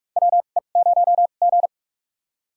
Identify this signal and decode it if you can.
Morse code, heard as a keyed tone, about 42 wpm, spaced as for 31 wpm, 700 Hz. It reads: WE0G
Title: USA Call Signs - 4 Characters - 2x Character Spacing - 45wpm